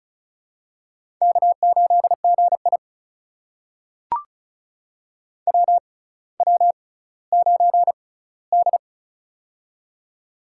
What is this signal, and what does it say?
Morse code, heard as a keyed tone, about 36 wpm, spaced as for 18 wpm, 700 Hz. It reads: K8GI WW9D
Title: USA Call Signs - 4 Characters - 6x Character Spacing - 35wpm